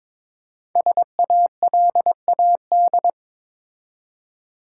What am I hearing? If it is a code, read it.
Morse code, heard as a keyed tone, about 22 wpm, 700 Hz. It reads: SALAD
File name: 5 Letter Words - First 3 Letters Given - 22wpm